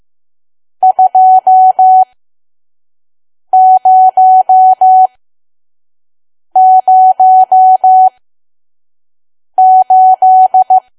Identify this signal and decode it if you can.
Morse code, heard as a keyed tone, about 15 wpm, spaced as for 5 wpm, 750 Hz. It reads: 2008
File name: Archive / 5 WPM files / PM